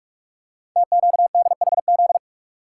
Morse code, heard as a keyed tone, about 44 wpm, 700 Hz. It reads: TQDHZ
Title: Code Groups - 5 Letter:Numbers - 45wpm